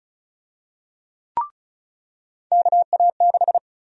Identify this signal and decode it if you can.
Morse code, heard as a keyed tone, about 34 wpm, 700 Hz. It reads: KA6
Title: USA Call Sign Prefixes - 35wpm